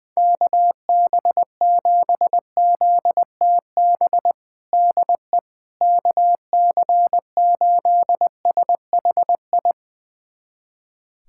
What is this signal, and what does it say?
Morse code, heard as a keyed tone, about 20 wpm, 700 Hz. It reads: KB7ZTB DE KC8SHI
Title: USA Callsign DE Callsign - 6 Characters - 20wpm